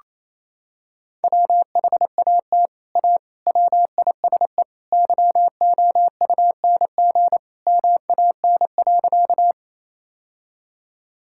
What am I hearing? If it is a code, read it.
Morse code, heard as a keyed tone, about 28 wpm, 700 Hz. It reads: WHAT A WISE YOUNG MAN.
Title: Sentences from Top 1600 Words 28wpm